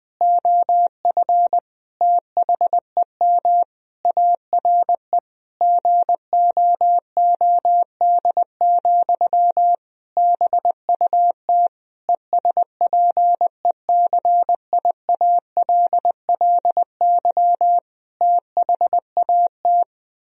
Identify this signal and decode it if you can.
Morse code, heard as a keyed tone, about 21 wpm, 700 Hz. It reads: OF THEM ARE GOOD, BUT ESPECIALLY THAT
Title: Sentences from Top 1000 Words - Review Part 4 - 20wpm